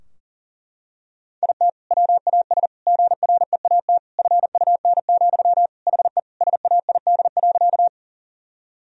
Morse code, heard as a keyed tone, about 40 wpm, 700 Hz. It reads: IT WAS GREAT FUN, HE SAID.
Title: Sentences from Top 1000 Words - Review Part 4 - 40wpm